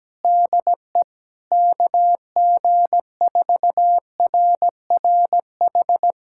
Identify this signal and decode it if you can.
Morse code, heard as a keyed tone, about 17 wpm, 700 Hz. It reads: DE KG4RRH
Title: USA Callsign DE Callsign - 6 Characters - 17wpm